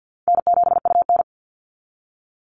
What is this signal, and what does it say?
Morse code, heard as a keyed tone, about 49 wpm, 700 Hz. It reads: N7UN
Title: NA SOTA Chaser Callsigns 50wpm